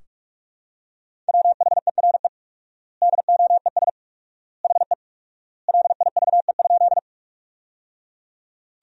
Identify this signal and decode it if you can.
Morse code, heard as a keyed tone, about 46 wpm, 700 Hz. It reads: WHERE DOES HE LIVE?
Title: 4x Word-Spacing - Sentences from Top 200 Words - 45wpm